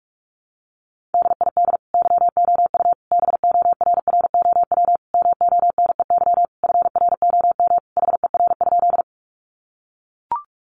Morse code, heard as a keyed tone, about 44 wpm, 700 Hz. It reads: DID YOU BORROW MONEY FROM HER?
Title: Sentences from Top 1500 Words 45wpm